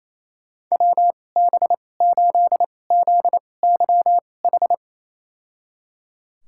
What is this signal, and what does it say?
Morse code, heard as a keyed tone, about 28 wpm, spaced as for 20 wpm, 700 Hz. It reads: WB8ZYH
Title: USA Call Signs - 2x Character Spacing - 28wpm